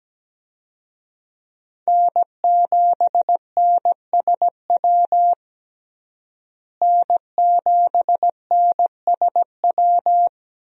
Morse code, heard as a keyed tone, about 17 wpm, 700 Hz. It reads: N7NSW N7NSW
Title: USA Call Signs - 5 Characters - Repeated 2x - 17wpm